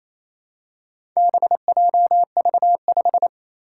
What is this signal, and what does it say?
Morse code, heard as a keyed tone, about 28 wpm, 700 Hz. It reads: BJV5
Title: Four Challenging Letter-Number Combinations 28wpm